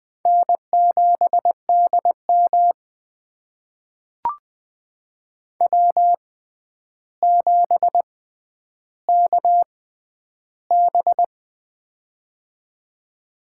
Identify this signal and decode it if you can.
Morse code, heard as a keyed tone, about 20 wpm, spaced as for 10 wpm, 700 Hz. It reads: N7DM W7KB
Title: USA Call Signs - 4 Characters - 6x Character Spacing - 20wpm